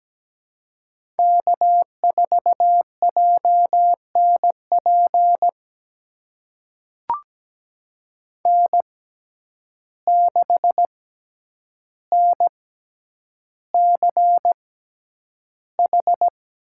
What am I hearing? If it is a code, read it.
Morse code, heard as a keyed tone, about 17 wpm, spaced as for 9 wpm, 700 Hz. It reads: K4JNP N6NCH
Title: USA Call Signs - 6x Character Spacing - 17wpm